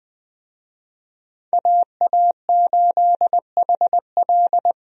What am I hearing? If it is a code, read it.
Morse code, heard as a keyed tone, about 20 wpm, 700 Hz. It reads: AA8HL
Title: USA Call Signs - 5 Characters - Repeated 3x - 20wpm